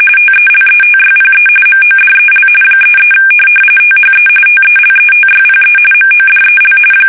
RTTY, about 45.45 baud. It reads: TELETYPE MODEL 33 THE TELETYPE MODEL
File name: fsk850 / teletype-corp